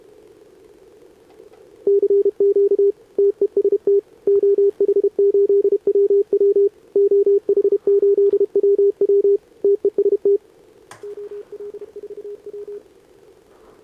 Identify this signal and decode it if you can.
Morse code, heard as a keyed tone, about 27 wpm, 410 Hz. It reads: CQ TEST OH8WW OH8WW TEST OL4W
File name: CQWWCW2025 / Sobota 15m ZD7VJ